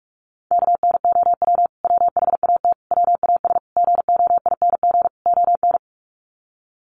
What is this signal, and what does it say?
Morse code, heard as a keyed tone, about 44 wpm, 700 Hz. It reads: KNOW WHAT WAS GOING ON
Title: Sets of 5 Words Encore2 45wpm